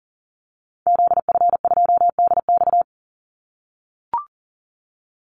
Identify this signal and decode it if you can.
Morse code, heard as a keyed tone, about 40 wpm, 700 Hz. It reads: ZF2DX
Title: Call Signs - 40wpm